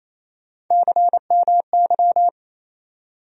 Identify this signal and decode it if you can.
Morse code, heard as a keyed tone, about 27 wpm, 700 Hz. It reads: CMY
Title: USA Call Sign Postfixes - 3 Letters 28wpm